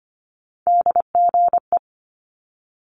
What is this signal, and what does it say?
Morse code, heard as a keyed tone, about 25 wpm, 700 Hz. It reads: DGE
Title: Multi-letter Phonemes Quick Fire 25wpm